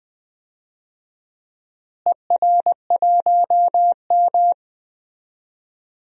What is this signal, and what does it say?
Morse code, heard as a keyed tone, about 20 wpm, 700 Hz. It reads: ER1M
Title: DXCC-100 Call Signs - 4 Characters - Repeated 3x - 20wpm